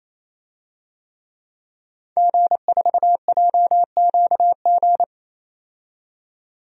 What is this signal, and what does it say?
Morse code, heard as a keyed tone, about 28 wpm, 700 Hz. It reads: G4JQG
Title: DXCC-100 Call Signs - 5 Characters - 28wpm